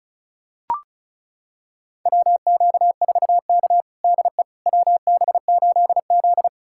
Morse code, heard as a keyed tone, about 36 wpm, 700 Hz. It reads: WQ4K DE WB8Z
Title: USA Callsign DE Callsign - 4 Characters - 35wpm